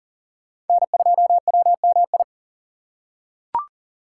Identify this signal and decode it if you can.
Morse code, heard as a keyed tone, about 37 wpm, 700 Hz. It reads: N2WMI
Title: USA Call Signs - 5 Characters - 2x Character Spacing - 40wpm